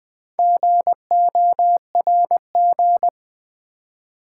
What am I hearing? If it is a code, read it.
Morse code, heard as a keyed tone, about 20 wpm, 700 Hz. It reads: GORG
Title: Build A Word - 4 Letters At A Time - 8-Letter Words - 20wpm